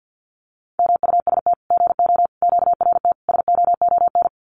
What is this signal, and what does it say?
Morse code, heard as a keyed tone, about 50 wpm, 700 Hz. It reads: MUST GO QRT SOON
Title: Common QSO Phrases 50wpm